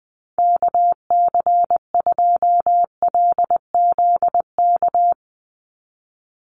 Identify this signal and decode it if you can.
Morse code, heard as a keyed tone, about 20 wpm, 700 Hz. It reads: KC2LZK
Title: USA Call Signs 20wpm